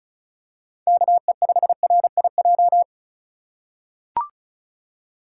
Morse code, heard as a keyed tone, about 36 wpm, 700 Hz. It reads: KE5RIJ
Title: USA Call Signs - 6 Characters - 4x Character Spacing - 35wpm